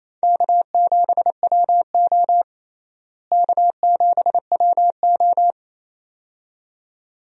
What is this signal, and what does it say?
Morse code, heard as a keyed tone, about 28 wpm, 700 Hz. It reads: K7WO K7WO
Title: USA Call Signs - 4 Characters - Repeated 2x - 28wpm